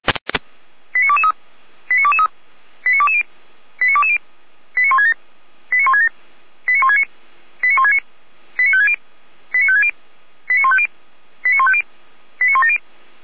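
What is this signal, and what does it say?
Five-tone selective call (ZVEI), heard as quick runs of five tones, tones of about 70 ms. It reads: 89223 89223 89220 89220 89167 89167 89169 89169 89560 89560 89150 89150 89170
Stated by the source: SDR receiver recording of ZVEI five-tone selective call, code 89121.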